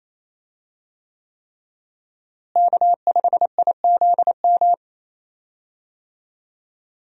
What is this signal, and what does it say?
Morse code, heard as a keyed tone, about 28 wpm, 700 Hz. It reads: K5IZM
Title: USA Call Signs - 5 Characters - Repeated 3x - 28wpm